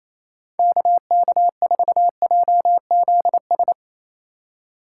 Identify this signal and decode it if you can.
Morse code, heard as a keyed tone, about 37 wpm, 700 Hz. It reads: KK4JZS